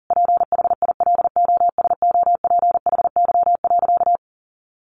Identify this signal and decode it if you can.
Morse code, heard as a keyed tone, about 40 wpm, 700 Hz. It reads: PHILOSOPHY.
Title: Sentences from Top 900 Words 40wpm